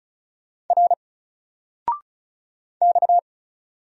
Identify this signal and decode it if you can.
Morse code, heard as a keyed tone, about 36 wpm, 700 Hz. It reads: R X
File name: Single-Letter-Number-35wpm